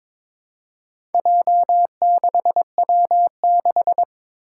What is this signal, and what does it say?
Morse code, heard as a keyed tone, about 22 wpm, 700 Hz. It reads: J6W6
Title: Four Challenging Letter-Number Combinations 22wpm